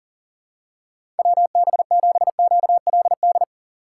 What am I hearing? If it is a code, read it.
Morse code, heard as a keyed tone, about 39 wpm, 700 Hz. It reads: WB7QLD
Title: USA Call Signs - 6 Characters - 8x Character Spacing - 40wpm